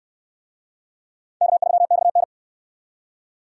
Morse code, heard as a keyed tone, about 69 wpm, 700 Hz. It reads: K3XN